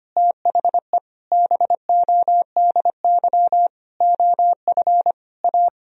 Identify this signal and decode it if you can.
Morse code, heard as a keyed tone, about 25 wpm, 700 Hz. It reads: THE BODY OF A
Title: Sentences from Top 1000 Words - Review Part 2 - 25wpm